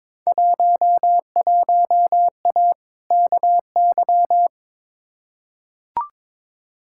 Morse code, heard as a keyed tone, about 22 wpm, 700 Hz. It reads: 11A KY
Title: Field Day Exchanges Encore - 22wpm